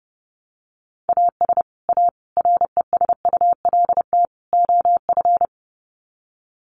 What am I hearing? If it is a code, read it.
Morse code, heard as a keyed tone, about 30 wpm, 700 Hz. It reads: AS A RESULT OF
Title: Sets of 4 Words 30wpm